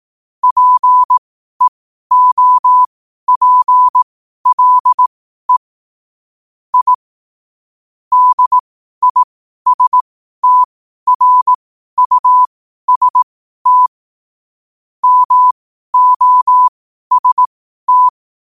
Morse code, heard as a keyed tone, about 18 wpm, spaced as for 12 wpm, 1000 Hz. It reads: PEOPLE I DISTRUST MOST
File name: qotdfiles / 12wpm